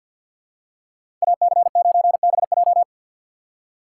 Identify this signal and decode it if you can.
Morse code, heard as a keyed tone, about 49 wpm, 700 Hz. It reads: AC9BJ
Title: USA Call Signs - 5 Characters - 4x Character Spacing - 50wpm